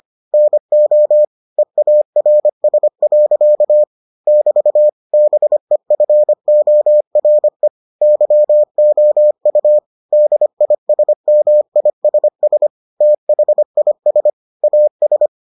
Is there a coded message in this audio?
Yes: Morse code from 0.3 to 15.3 s, heard as a keyed tone, about 25 wpm, 600 Hz. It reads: NO EARS. <BT> BEFORE YOU DISMISS THIS AS